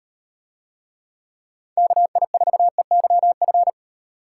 Morse code, heard as a keyed tone, about 38 wpm, 700 Hz. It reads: KI4EYF